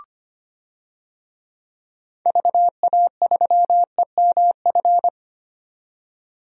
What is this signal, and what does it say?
Morse code, heard as a keyed tone, about 25 wpm, 700 Hz. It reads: VA3EMF
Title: DXCC-100 Call Signs - 6 Characters - 25wpm